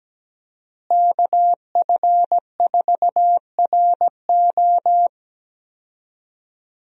Morse code, heard as a keyed tone, about 17 wpm, 700 Hz. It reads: KF4RO
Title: USA Call Signs - 5 Characters - Repeated 2x - 17wpm